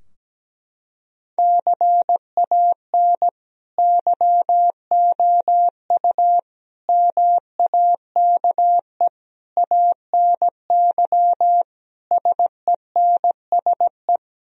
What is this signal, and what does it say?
Morse code, heard as a keyed tone, about 17 wpm, 700 Hz. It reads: CAN YOU MAKE ANY SENSE
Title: Sentences from Top 1000 Words - Review Part 4 - 17wpm